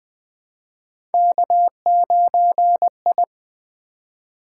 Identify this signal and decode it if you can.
Morse code, heard as a keyed tone, about 20 wpm, 700 Hz. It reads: K9I
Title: Three Letter-Number Combinations - 20wpm